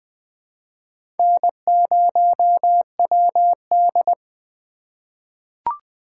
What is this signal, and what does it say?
Morse code, heard as a keyed tone, about 20 wpm, 700 Hz. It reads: N0WD
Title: USA Call Signs - 4 Characters - 2x Character Spacing - 20wpm